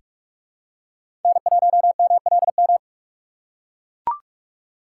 Morse code, heard as a keyed tone, about 43 wpm, 700 Hz. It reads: N1MLM
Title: USA Call Signs - 5 Characters - 3x Character Spacing - 45wpm